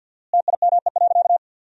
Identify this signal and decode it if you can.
Morse code, heard as a keyed tone, about 52 wpm, 700 Hz. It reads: TIME.
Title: 4x Word-Spacing - Sentences from Top 200 Words - 50wpm